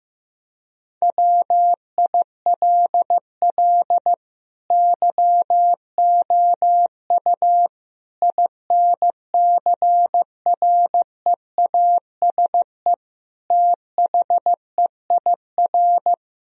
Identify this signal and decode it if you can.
Morse code, heard as a keyed tone, about 15 wpm, 700 Hz. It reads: WILL YOU INCREASE THEIR
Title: Sentences from Top 1000 Words - Review Part 4 - 15wpm